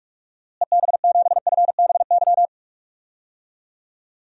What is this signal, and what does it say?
Morse code, heard as a keyed tone, about 46 wpm, 700 Hz. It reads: EB7FBY
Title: DXCC-100 Call Signs - 6 Characters - Repeated 3x - 45wpm